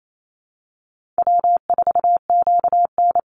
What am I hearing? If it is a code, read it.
Morse code, heard as a keyed tone, about 28 wpm, 700 Hz. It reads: W4QN
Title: Easy Call Signs Repeated 3x - 28wpm